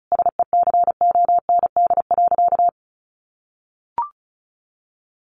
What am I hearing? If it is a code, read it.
Morse code, heard as a keyed tone, about 34 wpm, 700 Hz. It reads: SECOND.
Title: A Sound Of Thunder 35wpm